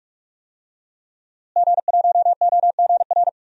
Morse code, heard as a keyed tone, about 46 wpm, 700 Hz. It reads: G1OGR
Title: DXCC-100 Call Signs - 5 Characters - Repeated 2x - 45wpm